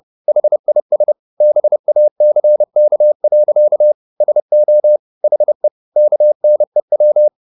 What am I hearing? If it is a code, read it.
Morse code, heard as a keyed tone, about 30 wpm, 600 Hz. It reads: HIS BACK. SO HE KNEW
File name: the old man and the sea / the old man and the sea 1 30wpm